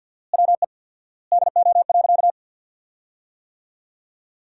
Morse code, heard as a keyed tone, about 50 wpm, 700 Hz. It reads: WE DO.
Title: 4x Word-Spacing - Sentences from Top 100 Words - 50wpm